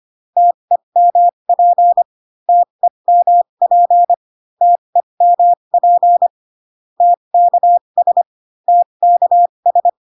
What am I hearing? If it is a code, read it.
Morse code, heard as a keyed tone, about 25 wpm, spaced as for 22 wpm, 700 Hz. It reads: TEMP TEMP TEMP TKS TKS
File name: CW common abbreviations alphabetical repeated 3 times 25 18wpm